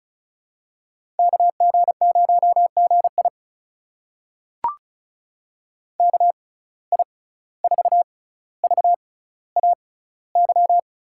Morse code, heard as a keyed tone, about 36 wpm, spaced as for 17 wpm, 700 Hz. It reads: KG0GI KI4VAY
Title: USA Call Signs - 6x Character Spacing - 35wpm